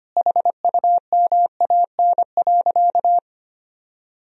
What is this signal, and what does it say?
Morse code, heard as a keyed tone, about 25 wpm, 700 Hz. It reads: HUMAN.
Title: Sentences from Top 1000 Words - Review Part 3 - 25wpm